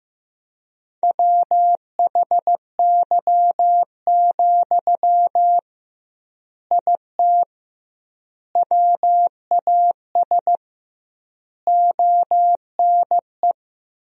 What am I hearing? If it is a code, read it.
Morse code, heard as a keyed tone, about 16 wpm, 700 Hz. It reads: WHY, IT WAS ONE
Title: 2x Word-Spacing - Sentences from Top 300 Words - 15wpm